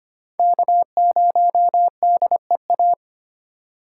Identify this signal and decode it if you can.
Morse code, heard as a keyed tone, about 25 wpm, 700 Hz. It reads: K0DEA
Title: USA Call Signs - 5 Characters - 2x Character Spacing - 25wpm